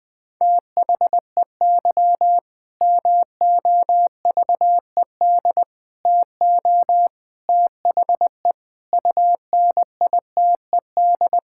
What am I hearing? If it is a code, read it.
Morse code, heard as a keyed tone, about 20 wpm, 700 Hz. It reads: THEY MOVED TO THE UNITED
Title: Sentences with 4-Word Phrases - Spoken Beginning - 20wpm